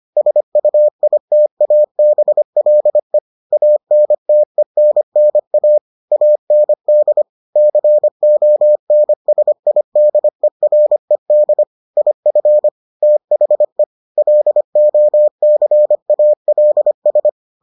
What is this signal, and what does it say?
Morse code, heard as a keyed tone, about 25 wpm, 600 Hz. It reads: SUITABLE ANTENNA AND CONSIDERED IF THE LOCALS